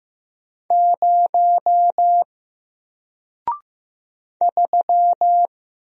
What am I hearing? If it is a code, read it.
Morse code, heard as a keyed tone, about 15 wpm, 700 Hz. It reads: 0 3
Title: Single Letter-Number 15wpm